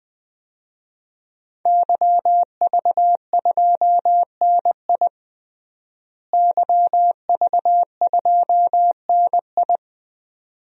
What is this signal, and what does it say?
Morse code, heard as a keyed tone, about 20 wpm, 700 Hz. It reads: YV2NI YV2NI
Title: DXCC-100 Call Signs - 5 Characters - Repeated 2x - 20wpm